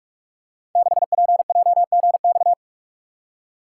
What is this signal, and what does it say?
Morse code, heard as a keyed tone, about 44 wpm, 700 Hz. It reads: 6PJGX
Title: Code Groups - 5 Letter:Numbers - 45wpm